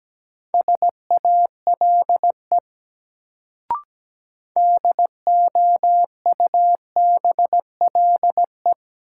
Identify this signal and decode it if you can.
Morse code, heard as a keyed tone, about 17 wpm, 700 Hz. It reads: SALE DOUBLE
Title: Sentences from Top 1400 Words - New Word Drill - 17wpm